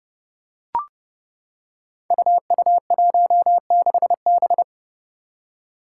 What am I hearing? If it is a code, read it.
Morse code, heard as a keyed tone, about 30 wpm, 700 Hz. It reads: UU16B
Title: Five Challenging Letter-Number Combinations 30wpm